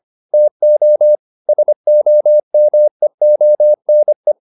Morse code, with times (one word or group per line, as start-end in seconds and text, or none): 0.3-1.1 TO
1.5-4.3 SOMEONE